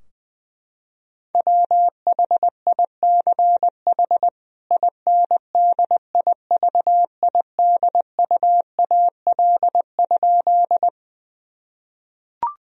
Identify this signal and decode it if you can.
Morse code, heard as a keyed tone, about 21 wpm, 700 Hz. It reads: WHICH INDIVIDUAL?
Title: Sentences from Top 1000 Words - Review Part 4 - 20wpm